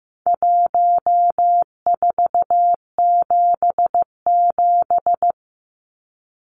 Morse code, heard as a keyed tone, about 15 wpm, 700 Hz. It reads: 1477
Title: Numbers - Contest Style - 4-Digit 15wpm